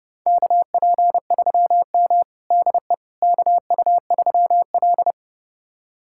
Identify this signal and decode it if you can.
Morse code, heard as a keyed tone, about 32 wpm, 700 Hz. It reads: KP3M DE KU3L
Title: USA Callsign DE Callsign - 4 Characters - 30wpm